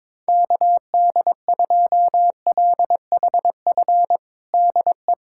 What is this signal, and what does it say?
Morse code, heard as a keyed tone, about 22 wpm, 700 Hz. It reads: KD2LHF DE
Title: USA Callsign DE Callsign - 6 Characters - 22wpm